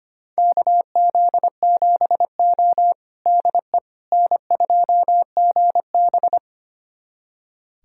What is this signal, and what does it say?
Morse code, heard as a keyed tone, about 25 wpm, 700 Hz. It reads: KZ7O DE N2GB
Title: USA Callsign DE Callsign - 4 Characters - 25wpm